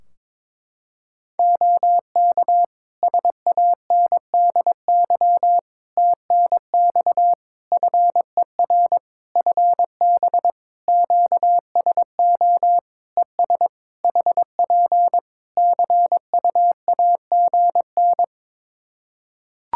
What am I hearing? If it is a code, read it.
Morse code, heard as a keyed tone, about 22 wpm, 700 Hz. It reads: OK SANDY TNX FER FB QSO ES HP CUAGN
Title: K1USN SST QSO Practice 22wpm